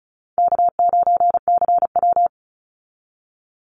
Morse code, heard as a keyed tone, about 34 wpm, 700 Hz. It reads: K9CW
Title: Call Signs Repeated 2x - 35wpm